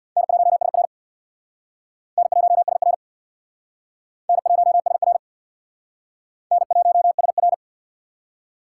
Morse code, heard as a keyed tone, about 65 wpm, 690 Hz. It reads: N1SR N1SR N1SR N1SR